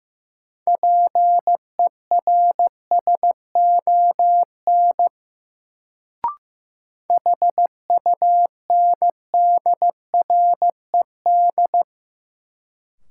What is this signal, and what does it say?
Morse code, heard as a keyed tone, about 15 wpm, 700 Hz. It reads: PERSON HUNDRED
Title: Sentences from Top 400 Words - New Word Drill - 15wpm